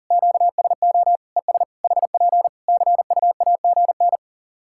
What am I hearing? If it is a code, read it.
Morse code, heard as a keyed tone, about 40 wpm, 700 Hz. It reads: QSO ES HP CUAGN
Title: Practice QSO - Exchange 4 - Ending - 40wpm